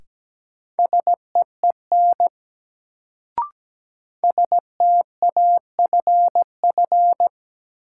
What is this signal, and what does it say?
Morse code, heard as a keyed tone, about 17 wpm, 700 Hz. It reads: SEEN STAFF
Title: Sets of 5 Words Encore - New Word Drill - 17wpm